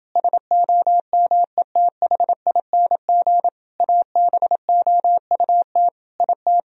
Morse code, heard as a keyed tone, about 29 wpm, 700 Hz. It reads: SOMETHING ABOUT IT